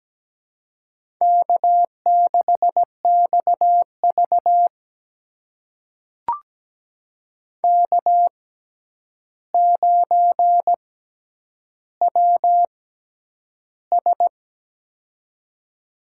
Morse code, heard as a keyed tone, about 17 wpm, spaced as for 9 wpm, 700 Hz. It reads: K6XV K9WS
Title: USA Call Signs - 4 Characters - 6x Character Spacing - 17wpm